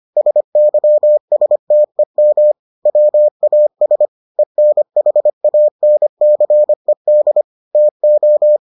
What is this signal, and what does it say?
Morse code, heard as a keyed tone, about 25 wpm, 600 Hz. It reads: SYSTEM WAS ENHANCED TO